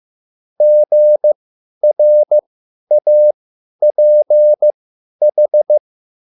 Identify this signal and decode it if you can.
Morse code, heard as a keyed tone, about 15 wpm, spaced as for 10 wpm, 600 Hz. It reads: GRAPH